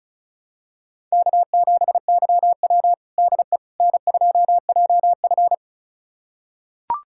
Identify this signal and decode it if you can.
Morse code, heard as a keyed tone, about 36 wpm, 700 Hz. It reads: K7YW DE N2JF
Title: USA Callsign DE Callsign - 4 Characters - 35wpm